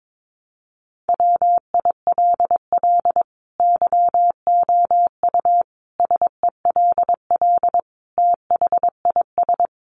Morse code, heard as a keyed tone, about 22 wpm, 700 Hz. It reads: WILL YOU SELL THIS
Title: Sentences from Top 1200 Words 22wpm